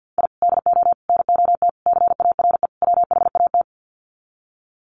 Morse code, heard as a keyed tone, about 50 wpm, 700 Hz. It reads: I DO NOT CARE WHAT
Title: Sets of 5 Words 50wpm